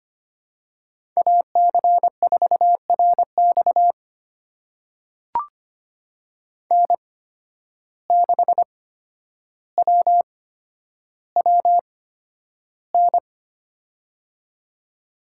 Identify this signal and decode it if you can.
Morse code, heard as a keyed tone, about 25 wpm, spaced as for 11 wpm, 700 Hz. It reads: AC4RX N6WWN
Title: USA Call Signs - 5 Characters - 8x Character Spacing - 25wpm